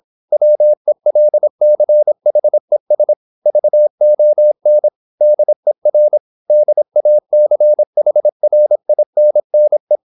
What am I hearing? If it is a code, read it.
Morse code, heard as a keyed tone, about 26 wpm, 600 Hz. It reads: WELCHES VON DER DACHRINNE